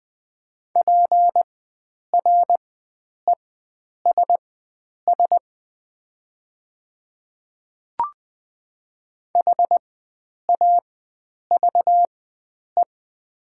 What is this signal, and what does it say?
Morse code, heard as a keyed tone, about 20 wpm, spaced as for 9 wpm, 700 Hz. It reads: PRESS HAVE
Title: 4x Character Spacing - Top 500 Words - 20wpm